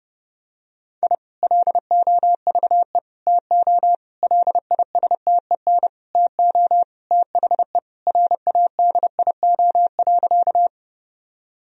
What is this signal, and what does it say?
Morse code, heard as a keyed tone, about 32 wpm, 700 Hz. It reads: I LOVE TO LISTEN TO THE RADIO.
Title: Sentences from Top 1000 Words - Review Part 3 - 30wpm